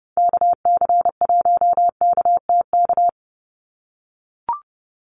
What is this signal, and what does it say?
Morse code, heard as a keyed tone, about 30 wpm, 700 Hz. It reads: KC1KTK
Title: USA Call Signs 30wpm